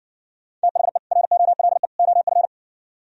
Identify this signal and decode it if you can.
Morse code, heard as a keyed tone, about 60 wpm, 700 Hz. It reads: THE ROLE OF